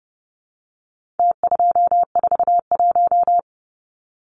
Morse code, heard as a keyed tone, about 30 wpm, 700 Hz. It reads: T241
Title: Numbers - Contest Style - 4-Digit 30wpm